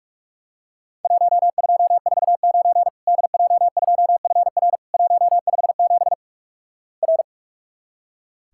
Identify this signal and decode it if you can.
Morse code, heard as a keyed tone, about 46 wpm, 700 Hz. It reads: 1249 DJ2FR 157 R
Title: Worked All Europe DX Contest - QTC Exchange - 45wpm